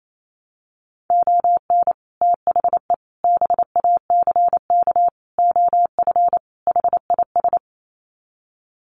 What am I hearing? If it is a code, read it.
Morse code, heard as a keyed tone, about 28 wpm, 700 Hz. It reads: ON THE BACK OF HIS
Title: Sets of 5 Words Encore 28wpm